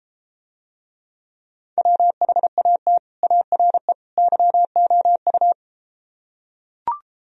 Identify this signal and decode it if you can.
Morse code, heard as a keyed tone, about 33 wpm, 700 Hz. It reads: WHAT ARE YOU